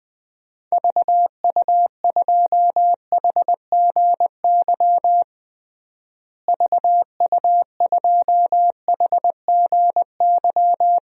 Morse code, heard as a keyed tone, about 20 wpm, 700 Hz. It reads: VU2HGY VU2HGY
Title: DXCC-100 Call Signs - 6 Characters - Repeated 3x - 20wpm